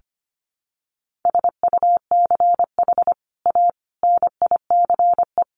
Morse code, heard as a keyed tone, about 25 wpm, 700 Hz. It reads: SUCH A NICE
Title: Sentences from Top 1100 Words 25wpm